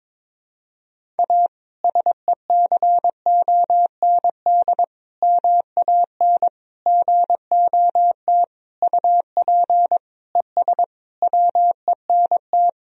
Morse code, heard as a keyed tone, about 22 wpm, 700 Hz. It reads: A SECOND MAN GOT UP ES WENT
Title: Sentences from Top 300 Words and Abbreviations 22wpm